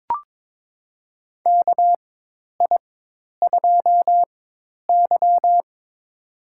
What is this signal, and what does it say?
Morse code, heard as a keyed tone, about 22 wpm, spaced as for 10 wpm, 700 Hz. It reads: KI2Y
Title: USA Call Signs - 4 Characters - 4x Character Spacing - 22wpm